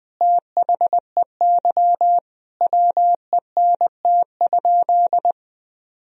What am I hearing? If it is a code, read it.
Morse code, heard as a keyed tone, about 20 wpm, 700 Hz. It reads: THEY WENT?
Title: Sentences from Top 1000 Words - Review Part 3 - 20wpm